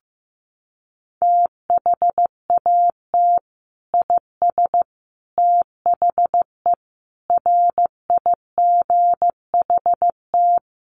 Morse code, heard as a keyed tone, about 15 wpm, 700 Hz. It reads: THAT IS THE RIGHT
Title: Sentences from Top 1400 Words 15wpm